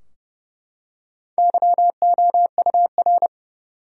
Morse code, heard as a keyed tone, about 30 wpm, 700 Hz. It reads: YOUR
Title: Top 100 Words Repeated 30wpm